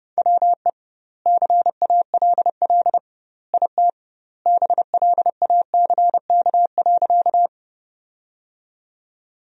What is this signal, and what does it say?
Morse code, heard as a keyed tone, about 30 wpm, 700 Hz. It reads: WE CALL IT BLACK.
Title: 2x Word-Spacing - Sentences from Top 400 Words - 30wpm